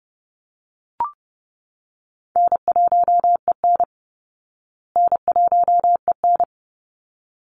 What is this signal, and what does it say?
Morse code, heard as a keyed tone, about 30 wpm, 700 Hz. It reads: N1EN N1EN
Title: Easy Call Signs Repeated 2x - 30wpm